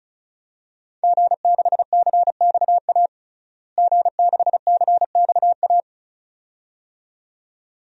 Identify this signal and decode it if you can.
Morse code, heard as a keyed tone, about 34 wpm, 700 Hz. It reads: G6CXA G6CXA
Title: DXCC-100 Call Signs - 5 Characters - Repeated 2x - 35wpm